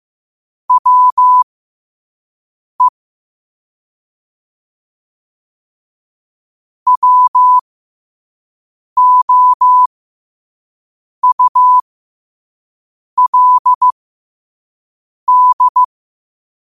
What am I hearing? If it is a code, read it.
Morse code, heard as a keyed tone, about 15 wpm, spaced as for 5 wpm, 1000 Hz. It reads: WE WOULD
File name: qotdfiles / 5wpm